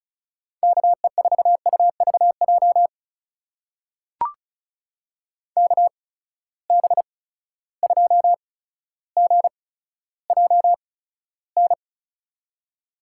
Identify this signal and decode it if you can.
Morse code, heard as a keyed tone, about 34 wpm, spaced as for 15 wpm, 700 Hz. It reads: KE4UVJ KB2GJN
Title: USA Call Signs - 6 Characters - 8x Character Spacing - 35wpm